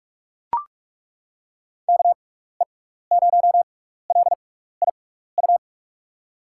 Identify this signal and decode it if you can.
Morse code, heard as a keyed tone, about 42 wpm, spaced as for 15 wpm, 700 Hz. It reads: KE0LIU
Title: USA Call Signs - 6x Character Spacing - 45wpm